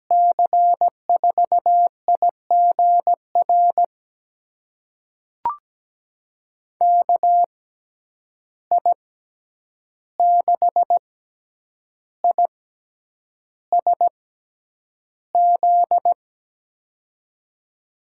Morse code, heard as a keyed tone, about 17 wpm, spaced as for 8 wpm, 700 Hz. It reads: C4IGR KI6ISZ
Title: USA Call Signs - 6 Characters - 6x Character Spacing - 17wpm